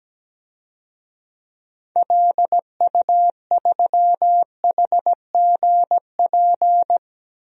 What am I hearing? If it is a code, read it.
Morse code, heard as a keyed tone, about 17 wpm, 700 Hz. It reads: LU3HGP
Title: DXCC-100 Call Signs - 6 Characters - 17wpm